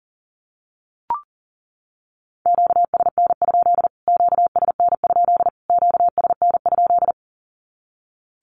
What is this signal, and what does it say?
Morse code, heard as a keyed tone, about 40 wpm, 700 Hz. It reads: QSN? QSN? QSN?
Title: Q-Codes Repeated 3x 40wpm